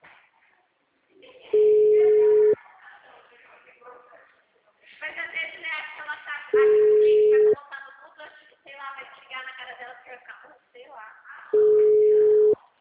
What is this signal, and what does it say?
Ringback tone (Europe, 425 Hz), with speech.